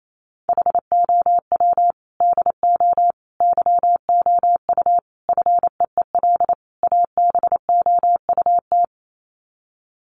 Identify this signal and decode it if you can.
Morse code, heard as a keyed tone, about 28 wpm, 700 Hz. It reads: HOW DO YOU FEEL ABOUT
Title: Sets of 5 Words 28wpm